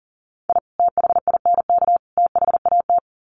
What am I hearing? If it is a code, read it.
Morse code, heard as a keyed tone, about 40 wpm, 700 Hz. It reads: I THINK THAT
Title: Sets of 3 Words 40wpm